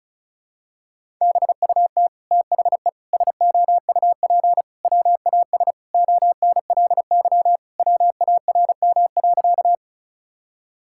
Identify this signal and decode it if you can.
Morse code, heard as a keyed tone, about 34 wpm, 700 Hz. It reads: BUT THE SOUP WAS ONLY WARM.
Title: Sentences from Top 1000 Words - Review Part 6 - 35wpm